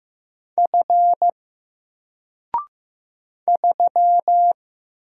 Morse code, heard as a keyed tone, about 15 wpm, 700 Hz. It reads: F 3
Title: Single Character (Letter:Number:Punctuation:Prosign) 15wpm